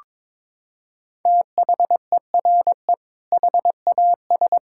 Morse code, heard as a keyed tone, about 22 wpm, 700 Hz. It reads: THERE HAS
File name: practice / 22wpm / Sets of 8 Words - 22wpm